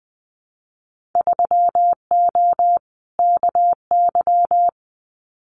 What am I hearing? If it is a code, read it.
Morse code, heard as a keyed tone, about 20 wpm, 700 Hz. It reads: 3O KY
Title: Winter Field Day Exchanges 20wpm